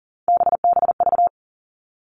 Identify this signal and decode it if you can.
Morse code, heard as a keyed tone, about 40 wpm, 700 Hz. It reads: BBV
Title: All Cut Numbers - 3-Digit 40wpm